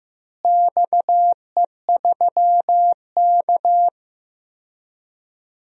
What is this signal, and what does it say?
Morse code, heard as a keyed tone, about 16 wpm, 700 Hz. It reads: XE3K
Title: DXCC-100 Call Signs - 4 Characters - Repeated 2x - 15wpm